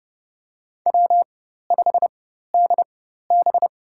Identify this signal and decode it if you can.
Morse code, heard as a keyed tone, about 30 wpm, spaced as for 14 wpm, 700 Hz. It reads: W5DB
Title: USA Call Signs - 4 Characters - 4x Character Spacing - 30wpm